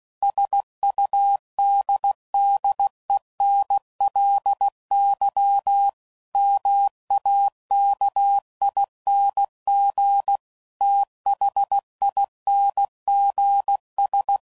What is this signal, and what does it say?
Morse code, heard as a keyed tone, about 16 wpm, 800 Hz. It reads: SUDDENLY MAKING THINGS
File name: CW Code Groups @15WPM / Technical Subjects / AND Gate